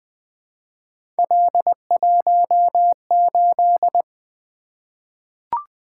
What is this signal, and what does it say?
Morse code, heard as a keyed tone, about 20 wpm, 700 Hz. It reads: L18
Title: Three Letter-Number Combinations - 20wpm